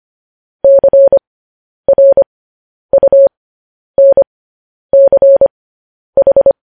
Morse code, heard as a keyed tone, about 25 wpm, spaced as for 10 wpm, 550 Hz. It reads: CRUNCH
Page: news headlines in morse code at 10 wpm